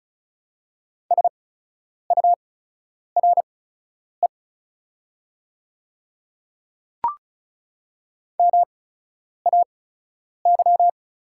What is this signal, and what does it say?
Morse code, heard as a keyed tone, about 36 wpm, spaced as for 10 wpm, 690 Hz. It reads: SURE MAY
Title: 8x Character Spacing - Top 300 Words - 35wpm